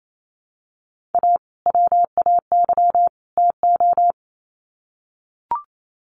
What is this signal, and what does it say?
Morse code, heard as a keyed tone, about 28 wpm, 700 Hz. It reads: A WAY TO
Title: Sets of 3 Words 28wpm